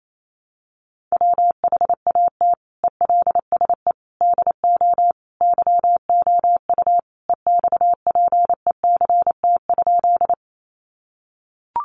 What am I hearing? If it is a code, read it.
Morse code, heard as a keyed tone, about 28 wpm, 700 Hz. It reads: WHAT ELSE DO YOU EXPECT?
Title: Sentences from Top 1100 Words 28wpm